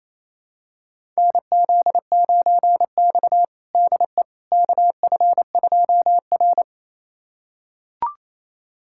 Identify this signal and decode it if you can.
Morse code, heard as a keyed tone, about 27 wpm, 700 Hz. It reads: NZ9X DE KF2R
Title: USA Callsign DE Callsign - 4 Characters - 28wpm